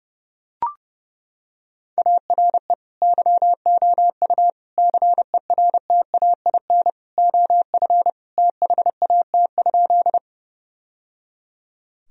Morse code, heard as a keyed tone, about 30 wpm, 700 Hz. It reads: ARE YOU CERTAIN OF THAT?
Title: Sentences from Top 1000 Words - Review Part 2 - 30wpm